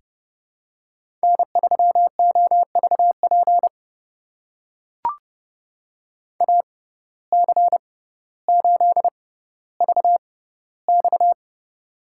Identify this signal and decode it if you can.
Morse code, heard as a keyed tone, about 28 wpm, spaced as for 15 wpm, 700 Hz. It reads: N3OVP AC8VX
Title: USA Call Signs - 5 Characters - 6x Character Spacing - 30wpm